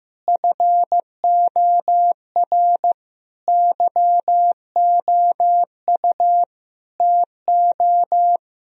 Morse code, heard as a keyed tone, about 15 wpm, 700 Hz. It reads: FOR YOU TO
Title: Sentences from Top 1000 Words - Review Part 3 - 15wpm